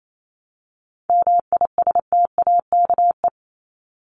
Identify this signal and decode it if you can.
Morse code, heard as a keyed tone, about 28 wpm, 700 Hz. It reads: MISTAKE
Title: Top 1501-1750 Words 28wpm